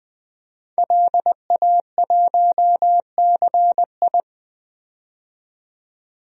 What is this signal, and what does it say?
Morse code, heard as a keyed tone, about 20 wpm, 700 Hz. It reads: LA1CI
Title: DXCC-100 Call Signs - 5 Characters - 20wpm